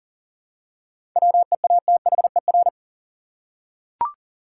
Morse code, heard as a keyed tone, about 40 wpm, 700 Hz. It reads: WEATHER
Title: Seven Letter Words - 40wpm